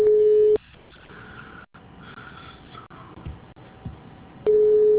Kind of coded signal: telephone call progress tone